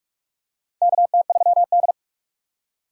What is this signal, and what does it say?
Morse code, heard as a keyed tone, about 43 wpm, 700 Hz. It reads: KT3D
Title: USA Call Signs - 4 Characters - 2x Character Spacing - 45wpm